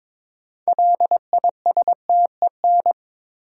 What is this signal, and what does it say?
Morse code, heard as a keyed tone, about 23 wpm, 700 Hz. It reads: LISTEN
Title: Sentences from Top 400 Words - New Word Drill - 22wpm